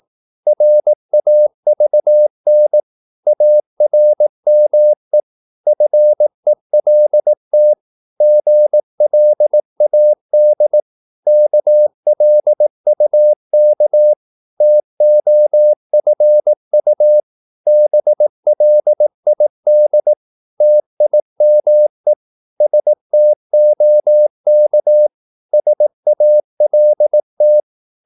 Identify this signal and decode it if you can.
Morse code, heard as a keyed tone, about 18 wpm, 600 Hz. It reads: RAVN ARME FELT GLAD KLUK TOFU BLID TIME STOK SALT
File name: Korte ord DK 18wpm